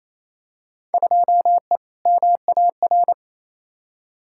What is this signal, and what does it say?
Morse code, heard as a keyed tone, about 28 wpm, 700 Hz. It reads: 2E MAR
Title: Field Day Exchanges Encore - 28wpm